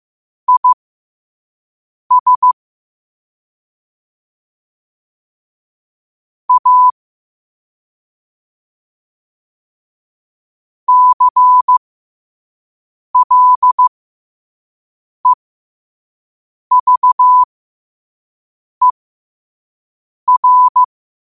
Morse code, heard as a keyed tone, about 15 wpm, spaced as for 5 wpm, 1000 Hz. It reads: IS A CLEVER
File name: qotdfiles / 5wpm